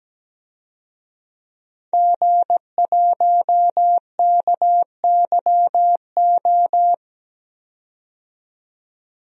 Morse code, heard as a keyed tone, about 17 wpm, 700 Hz. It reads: G1KYO
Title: DXCC-100 Call Signs - 5 Characters - 17wpm